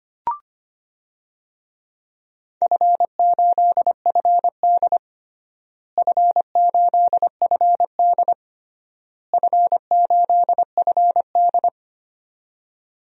Morse code, heard as a keyed tone, about 25 wpm, 700 Hz. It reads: F8FD F8FD F8FD
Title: DXCC-100 Call Signs - 4 Characters - Repeated 3x - 25wpm